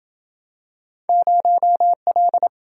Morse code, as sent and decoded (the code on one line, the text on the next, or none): ----- .-..
0L